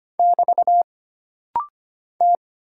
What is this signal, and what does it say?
Morse code, heard as a keyed tone, about 25 wpm, 700 Hz. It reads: <BT> T
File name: Single Character (Letter:Number:Punctuation:Prosign) - Rapid-Fire - 25wpm